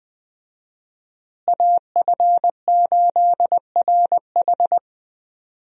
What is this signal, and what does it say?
Morse code, heard as a keyed tone, about 20 wpm, 700 Hz. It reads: AF8RH
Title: USA Call Signs - 5 Characters - Repeated 2x - 20wpm